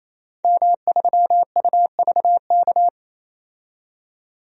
Morse code, heard as a keyed tone, about 28 wpm, 700 Hz. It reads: M3UVK
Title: DXCC-100 Call Signs - 5 Characters - 28wpm